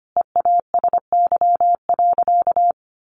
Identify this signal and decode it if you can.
Morse code, heard as a keyed tone, about 25 wpm, 700 Hz. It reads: EASY.
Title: Sentences from Top 1500 Words 25wpm